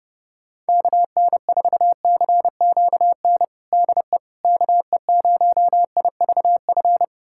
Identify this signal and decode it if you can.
Morse code, heard as a keyed tone, about 31 wpm, 700 Hz. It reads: KN4CQN DE KE0IVF
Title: USA Callsign DE Callsign - 6 Characters - 30wpm